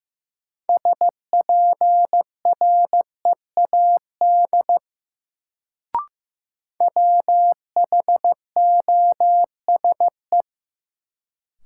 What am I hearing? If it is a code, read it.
Morse code, heard as a keyed tone, about 15 wpm, 700 Hz. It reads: SPREAD WHOSE
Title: Sentences from Top 1200 Words - New Word Drill Alt - 15wpm